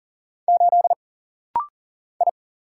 Morse code, heard as a keyed tone, about 40 wpm, 700 Hz. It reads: 8 I
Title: Single Letter-Number - Rapid-Fire 40wpm